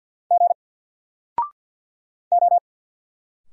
Morse code, heard as a keyed tone, about 53 wpm, 700 Hz. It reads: G O
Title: Single Letter-Number 50wpm